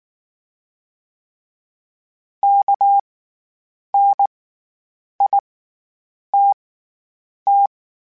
Morse code, heard as a keyed tone, about 19 wpm, spaced as for 8 wpm, 800 Hz. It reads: KNITT
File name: Special INGs / Special INGs-mp3s / SlowToFaster ing @18